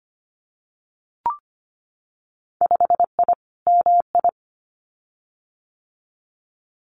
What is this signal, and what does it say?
Morse code, heard as a keyed tone, about 25 wpm, 700 Hz. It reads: 5I MI
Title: Winter Field Day Exchanges Encore 25wpm